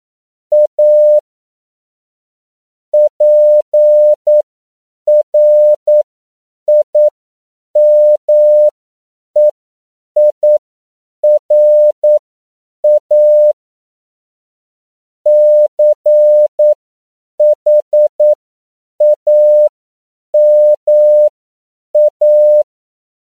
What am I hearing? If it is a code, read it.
Morse code, heard as a keyed tone, about 9 wpm, spaced as for 7 wpm, 600 Hz. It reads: A PRIMEIRA CHAMA
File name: CW / 10PPM